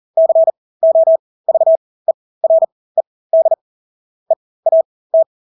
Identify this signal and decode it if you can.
Morse code, heard as a keyed tone, about 40 wpm, spaced as for 20 wpm, 650 Hz. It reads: COVERED EAT
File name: Engelske ord 40-20 wpm